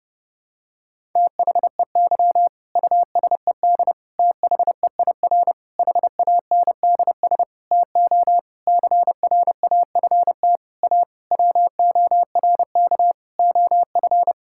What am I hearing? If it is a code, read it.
Morse code, heard as a keyed tone, about 30 wpm, 700 Hz. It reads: THEY USED THEIR HANDS TO CRAFT A WORK OF
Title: Sentences from Top 1000 Words - Review Part 6 - 30wpm